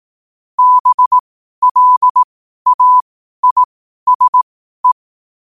Morse code, heard as a keyed tone, about 18 wpm, spaced as for 12 wpm, 1000 Hz. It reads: BLAISE
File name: qotdfiles / 12wpm